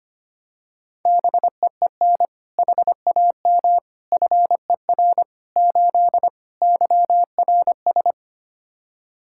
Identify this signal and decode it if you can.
Morse code, heard as a keyed tone, about 25 wpm, 700 Hz. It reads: BEEN HAM FER 8 YRS
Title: Practice QSO - Exchange 3 - More Info - 25wpm